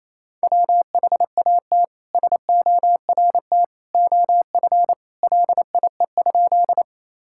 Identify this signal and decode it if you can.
Morse code, heard as a keyed tone, about 28 wpm, 700 Hz. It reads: WHAT SORT OF LIE?
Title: Sentences from Top 1600 Words 28wpm